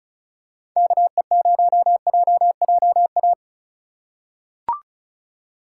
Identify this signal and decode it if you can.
Morse code, heard as a keyed tone, about 36 wpm, 700 Hz. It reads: KE0JJA
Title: USA Call Signs - 6 Characters - 2x Character Spacing - 35wpm